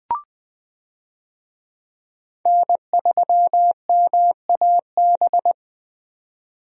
Morse code, heard as a keyed tone, about 20 wpm, 700 Hz. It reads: N3MAB
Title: USA Call Signs - 5 Characters - 20wpm